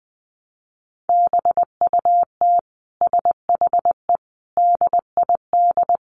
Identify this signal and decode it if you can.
Morse code, heard as a keyed tone, about 20 wpm, 700 Hz. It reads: BUT SHE DID
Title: Sentences from Top 600 Words 20wpm